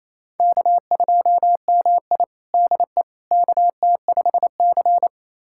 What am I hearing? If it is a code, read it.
Morse code, heard as a keyed tone, about 28 wpm, 700 Hz. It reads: K2MI DE KT5C
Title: USA Callsign DE Callsign - 4 Characters - 28wpm